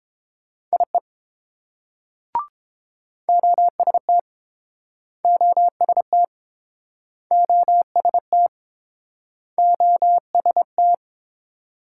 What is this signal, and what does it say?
Morse code, heard as a keyed tone, about 31 wpm, 700 Hz. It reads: IE OST OST OST OST